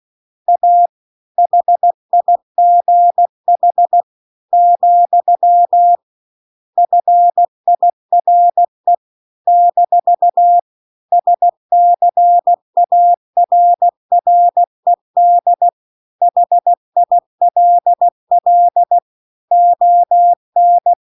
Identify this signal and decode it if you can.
Morse code, heard as a keyed tone, about 16 wpm, 700 Hz. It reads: A HIGH , FIRE - SCARRED HILL ON